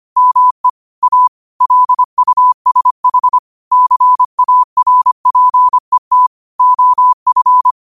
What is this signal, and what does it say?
Morse code, heard as a keyed tone, about 25 wpm, 1000 Hz. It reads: ME A LUSH CARPET OF
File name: qotdfiles / 25wpm